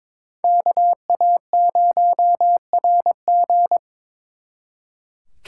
Morse code, heard as a keyed tone, about 26 wpm, 700 Hz. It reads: KA0RG
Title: Speed Racing - USA 2x2 Call Signs 22wpm